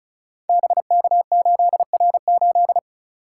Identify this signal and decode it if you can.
Morse code, heard as a keyed tone, about 36 wpm, 700 Hz. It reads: BK8R8
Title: Code Groups - 5 Letter:Numbers - 35wpm